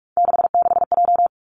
Morse code, heard as a keyed tone, about 46 wpm, 700 Hz. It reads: 66J
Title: Three Challenging Letter-Number Combinations 45wpm